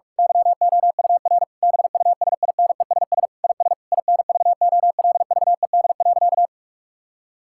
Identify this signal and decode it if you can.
Morse code, heard as a keyed tone, about 41 wpm, 700 Hz. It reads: YOUR BUSINESS IS INVOLVED.
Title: Sentences from Top 1000 Words - Review Part 5 - 45wpm